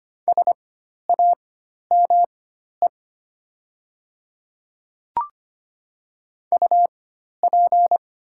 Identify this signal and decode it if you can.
Morse code, heard as a keyed tone, about 25 wpm, spaced as for 12 wpm, 700 Hz. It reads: SAME UP
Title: 4x Character Spacing - Top 200 Words - 25wpm